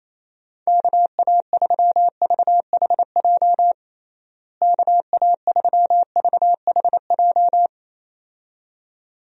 Morse code, heard as a keyed tone, about 28 wpm, 700 Hz. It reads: KA3VHJ KA3VHJ
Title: USA Call Signs - 6 Characters - Repeated 2x - 28wpm